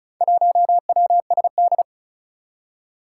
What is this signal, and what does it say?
Morse code, heard as a keyed tone, about 36 wpm, 700 Hz. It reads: 1WSD
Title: Four Challenging Letter-Number Combinations 35wpm